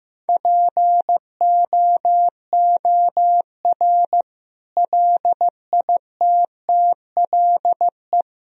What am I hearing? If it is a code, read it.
Morse code, heard as a keyed tone, about 15 wpm, 700 Hz. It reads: POOR LITTLE
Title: Sentences from Top 1000 Words - Review Part 3 - 15wpm